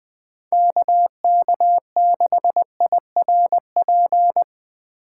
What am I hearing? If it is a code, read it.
Morse code, heard as a keyed tone, about 20 wpm, 700 Hz. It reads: KK6IRP
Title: USA Call Signs - 6 Characters - 2x Character Spacing - 20wpm